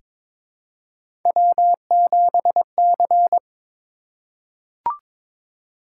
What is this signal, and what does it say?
Morse code, heard as a keyed tone, about 23 wpm, 700 Hz. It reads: W7C
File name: practice / 22wpm / Three Letter-Number Combinations - 22wpm